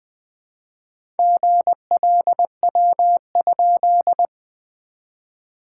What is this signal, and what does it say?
Morse code, heard as a keyed tone, about 20 wpm, 690 Hz. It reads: GLW?
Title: SOTA QSO Practice 20wpm